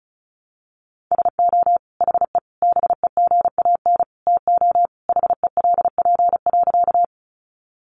Morse code, heard as a keyed tone, about 34 wpm, 700 Hz. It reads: SO HE BEGAN TO HELP.
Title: Sentences from Top 300 Words 35wpm